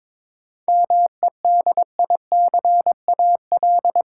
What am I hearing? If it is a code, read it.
Morse code, heard as a keyed tone, about 21 wpm, 700 Hz. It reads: MEDICAL
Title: Sentences from Top 800 Words - New Word Drill - 22wpm